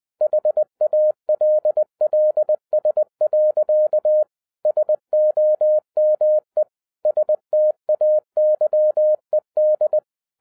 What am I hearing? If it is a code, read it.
Morse code, heard as a keyed tone, about 20 wpm, 610 Hz. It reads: HALLS. SOME STAYED
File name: 2011-07-mp3 / n33 THE 3.11 2 20wpm